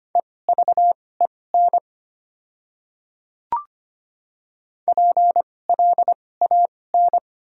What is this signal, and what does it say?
Morse code, heard as a keyed tone, about 25 wpm, spaced as for 18 wpm, 700 Hz. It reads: EVEN PLAN
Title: 2x Character Spacing - Top 500 Words - 25wpm